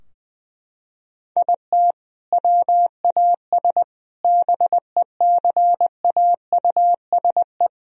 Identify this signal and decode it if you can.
Morse code, heard as a keyed tone, about 21 wpm, 700 Hz. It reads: IT WAS BECAUSE
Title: Sentences from Top 1000 Words - Review Part 4 - 20wpm